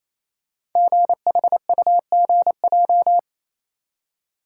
Morse code, heard as a keyed tone, about 28 wpm, 700 Hz. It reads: GHUGJ
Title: Code Groups - 5 Letters - 28wpm